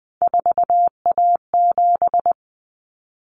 Morse code, heard as a keyed tone, about 20 wpm, 700 Hz. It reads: 4A7
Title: Common Cut Numbers - 3-Digit 20wpm